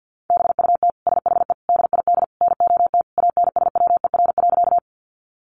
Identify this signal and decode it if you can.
Morse code, heard as a keyed tone, about 50 wpm, 700 Hz. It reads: BUT SHE DID NOT ANSWER.
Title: Sentences from Top 600 Words 50wpm